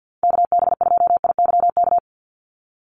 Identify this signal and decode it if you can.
Morse code, heard as a keyed tone, about 50 wpm, 700 Hz. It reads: KB2IYK
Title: USA Call Signs 50wpm